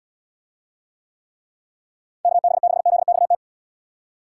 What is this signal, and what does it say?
Morse code, heard as a keyed tone, about 76 wpm, 700 Hz. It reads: KB6ZXT